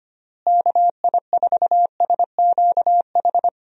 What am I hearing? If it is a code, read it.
Morse code, heard as a keyed tone, about 25 wpm, 700 Hz. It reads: KI4SQH
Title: USA Call Signs - 6 Characters - Repeated 3x - 25wpm